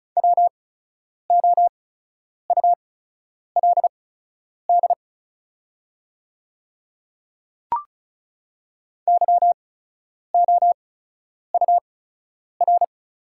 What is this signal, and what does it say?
Morse code, heard as a keyed tone, about 36 wpm, spaced as for 10 wpm, 690 Hz. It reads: WOULD YOUR
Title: 8x Character Spacing - Top 200 Words - 35wpm